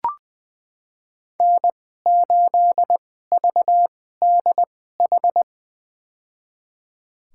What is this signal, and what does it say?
Morse code, heard as a keyed tone, about 20 wpm, spaced as for 14 wpm, 700 Hz. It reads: N8VDH
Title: USA Call Signs - 5 Characters - 2x Character Spacing - 20wpm